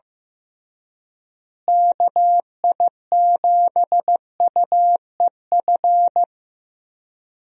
Morse code, heard as a keyed tone, about 16 wpm, 700 Hz. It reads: KI7UEF
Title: USA Call Signs - 6 Characters - Repeated 3x - 15wpm